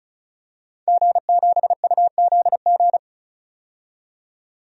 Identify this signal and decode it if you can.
Morse code, heard as a keyed tone, about 34 wpm, 700 Hz. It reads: G7UZG
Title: DXCC-100 Call Signs - 5 Characters - 35wpm